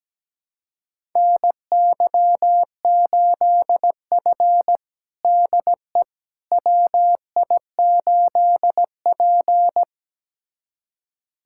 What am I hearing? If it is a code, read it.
Morse code, heard as a keyed tone, about 17 wpm, 700 Hz. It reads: NY8F DE WI8P
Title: USA Callsign DE Callsign - 4 Characters - 17wpm